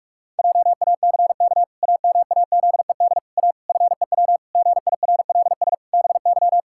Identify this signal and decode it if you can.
Morse code, heard as a keyed tone, about 46 wpm, 700 Hz. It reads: JACK AMAZED A FEW GIRLS BY
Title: Pangram Challenge v2 - 45wpm